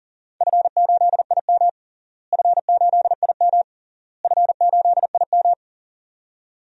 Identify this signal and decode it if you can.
Morse code, heard as a keyed tone, about 40 wpm, 700 Hz. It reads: F8IM F8IM F8IM
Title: DXCC-100 Call Signs - 4 Characters - Repeated 3x - 40wpm